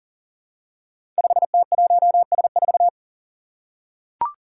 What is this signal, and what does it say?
Morse code, heard as a keyed tone, about 39 wpm, 700 Hz. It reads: 5T1S4
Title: Five Challenging Letter-Number Combinations 40wpm